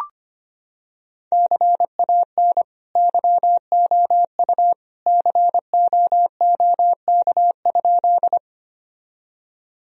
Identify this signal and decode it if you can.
Morse code, heard as a keyed tone, about 25 wpm, 700 Hz. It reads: CAN YOU COOK?
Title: Sentences from Top 1000 Words - Review Part 4 - 25wpm